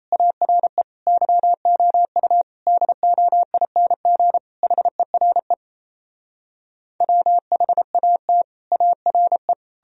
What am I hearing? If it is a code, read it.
Morse code, heard as a keyed tone, about 37 wpm, 700 Hz. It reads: ARE YOU DOING HERE WHAT ARE